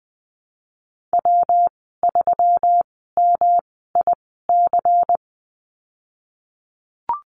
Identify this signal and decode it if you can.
Morse code, heard as a keyed tone, about 19 wpm, spaced as for 14 wpm, 700 Hz. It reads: W3MIC
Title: 219 Lesson 35g - Number 0 - Call Signs 20wpm